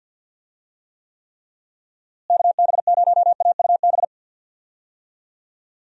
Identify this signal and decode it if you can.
Morse code, heard as a keyed tone, about 46 wpm, 700 Hz. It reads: KB0AUB
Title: USA Call Signs - 6 Characters - Repeated 3x - 50wpm